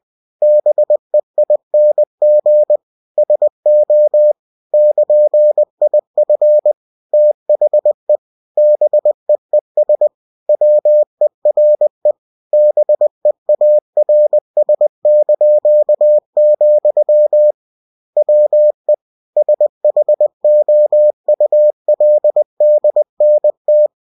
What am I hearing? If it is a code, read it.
Morse code, heard as a keyed tone, about 20 wpm, 600 Hz. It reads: BEING SO <KN>IF THE BEES WERE BEARS), WE SHOULDNT
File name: cw-books / winnie-the-pooh / winnie-20wpm-page-0003